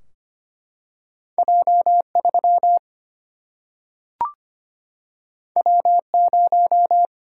Morse code, heard as a keyed tone, about 25 wpm, 700 Hz. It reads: J3 W0
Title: Single-Letter Single-Number 25wpm